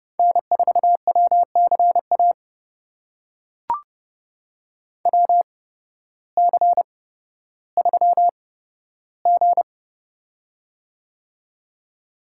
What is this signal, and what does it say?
Morse code, heard as a keyed tone, about 30 wpm, spaced as for 14 wpm, 700 Hz. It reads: N4WCA WC3G
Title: USA Call Signs - 8x Character Spacing - 30wpm